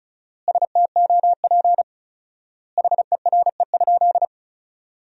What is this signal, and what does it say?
Morse code, heard as a keyed tone, about 36 wpm, 700 Hz. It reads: STOP HERE?
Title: 4x Word-Spacing - Sentences from Top 300 Words - 35wpm